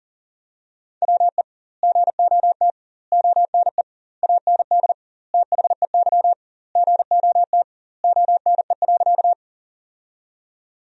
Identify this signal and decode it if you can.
Morse code, heard as a keyed tone, about 40 wpm, 700 Hz. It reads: WE GOT ONE AND THEY GOT ONE.
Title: 2x Word-Spacing - Sentences from Top 300 Words - 40wpm